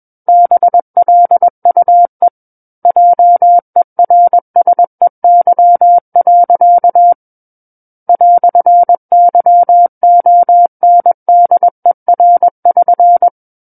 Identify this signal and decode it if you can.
Morse code, heard as a keyed tone, about 21 wpm, 700 Hz. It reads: BLUE JERSEY. "YONDER<SN>
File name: Book One / 12-Chapter 12 - 21 WPM